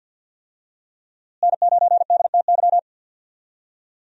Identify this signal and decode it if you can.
Morse code, heard as a keyed tone, about 52 wpm, 700 Hz. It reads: N9DTY
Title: USA Call Signs - 5 Characters - Repeated 3x - 50wpm